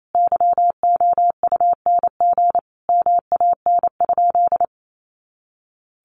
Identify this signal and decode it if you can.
Morse code, heard as a keyed tone, about 28 wpm, 700 Hz. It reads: YOUNG MAN?
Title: Sentences from Top 400 Words 28wpm